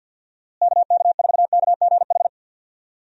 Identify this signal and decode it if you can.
Morse code, heard as a keyed tone, about 50 wpm, 700 Hz. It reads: KK4KGH
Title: USA Call Signs - 6 Characters - Repeated 2x - 50wpm